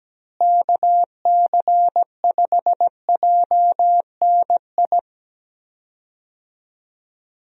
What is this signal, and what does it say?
Morse code, heard as a keyed tone, about 17 wpm, 700 Hz. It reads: KC5JNI
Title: USA Call Signs - 6 Characters - Repeated 3x - 17wpm